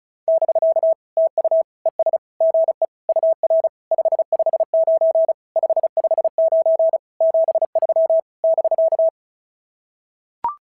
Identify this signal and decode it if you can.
Morse code, heard as a keyed tone, about 36 wpm, 690 Hz. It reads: <BK> TU ES GE UR 559 559 73 <BK>
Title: SOTA QSO Practice 35wpm